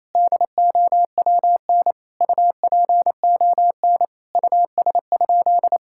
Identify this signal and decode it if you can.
Morse code, heard as a keyed tone, about 28 wpm, 700 Hz. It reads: DOWN UPON US?
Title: Sentences from Top 1600 Words 28wpm